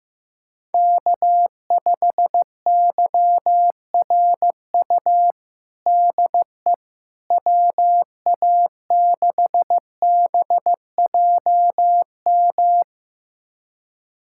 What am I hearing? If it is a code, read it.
Morse code, heard as a keyed tone, about 15 wpm, 700 Hz. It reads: K5YRU DE WA6BJM
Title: USA Callsign DE Callsign 15wpm